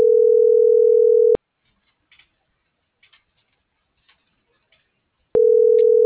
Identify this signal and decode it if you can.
Ringback tone (North America), 440 + 480 Hz.